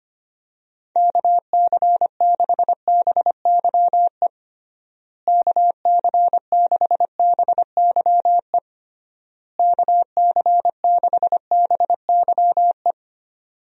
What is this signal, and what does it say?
Morse code, heard as a keyed tone, about 25 wpm, 700 Hz. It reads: KC6BYE KC6BYE KC6BYE
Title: USA Call Signs - 6 Characters - Repeated 3x - 25wpm